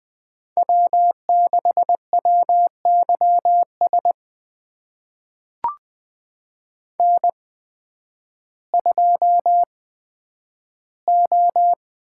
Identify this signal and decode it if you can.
Morse code, heard as a keyed tone, about 20 wpm, spaced as for 11 wpm, 700 Hz. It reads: W6WYS N2O
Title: USA Call Signs - 8x Character Spacing - 20wpm